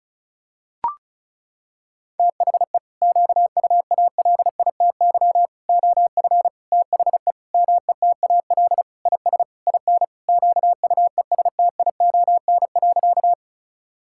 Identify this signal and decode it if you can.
Morse code, heard as a keyed tone, about 34 wpm, 700 Hz. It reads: THE QUALITY OF THE METAL IS IN QUESTION.
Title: Sentences from Top 1000 Words - Review Part 6 - 35wpm